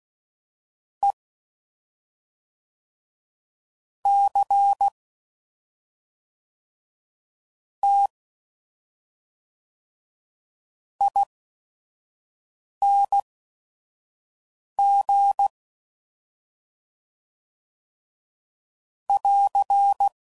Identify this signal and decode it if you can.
Morse code, heard as a keyed tone, about 16 wpm, spaced as for 5 wpm, 800 Hz. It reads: E C T ING <AR>
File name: Suffixes / Special ING's / SlowToFaster ING's